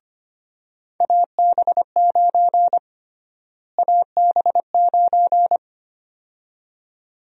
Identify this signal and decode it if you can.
Morse code, heard as a keyed tone, about 25 wpm, 700 Hz. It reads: AB9 AB9
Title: USA Call Sign Prefixes - Repeated 2x - 25wpm